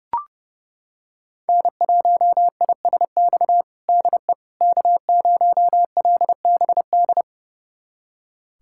Morse code, heard as a keyed tone, about 31 wpm, 700 Hz. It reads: N1ISX DE K0LBD
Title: USA Callsign DE Callsign - 5 Characters - 30wpm